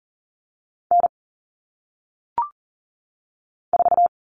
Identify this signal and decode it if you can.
Morse code, heard as a keyed tone, about 40 wpm, 700 Hz. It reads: N 4
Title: All Cut Numbers - 1-Digit 40wpm